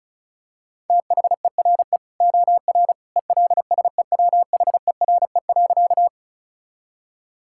Morse code, heard as a keyed tone, about 36 wpm, 700 Hz. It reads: THERE OR ELSEWHERE.
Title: Sentences from Top 1600 Words 35wpm